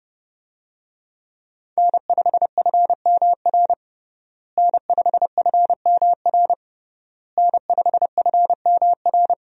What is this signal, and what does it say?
Morse code, heard as a keyed tone, about 30 wpm, 700 Hz. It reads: N5FMR N5FMR N5FMR
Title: USA Call Signs - 5 Characters - Repeated 3x - 30wpm